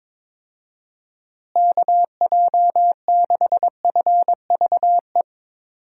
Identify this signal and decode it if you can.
Morse code, heard as a keyed tone, about 22 wpm, 700 Hz. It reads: KJ6FVE
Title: USA Call Signs - 6 Characters - Repeated 3x - 22wpm